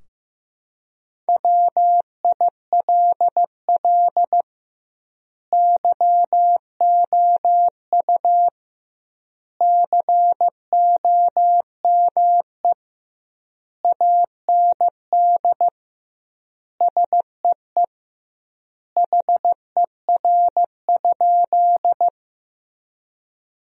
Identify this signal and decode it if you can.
Morse code, heard as a keyed tone, about 14 wpm, 700 Hz. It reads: WILL YOU COME AND SEE HER?
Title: 2x Word-Spacing - Sentences from Top 100 Words - 15wpm